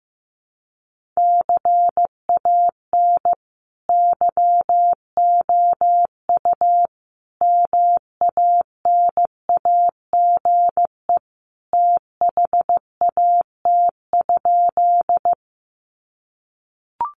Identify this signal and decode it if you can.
Morse code, heard as a keyed tone, about 15 wpm, 700 Hz. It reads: CAN YOU MANAGE THAT?
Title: Sentences from Top 1400 Words 15wpm